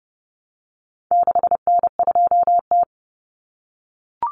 Morse code, heard as a keyed tone, about 30 wpm, 700 Hz. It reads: 6N2T
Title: Common Cut Numbers - 4-Digit 30wpm